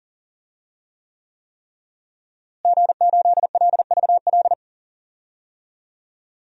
Morse code, heard as a keyed tone, about 40 wpm, 700 Hz. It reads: G8LVL
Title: DXCC-100 Call Signs - 5 Characters - Repeated 3x - 40wpm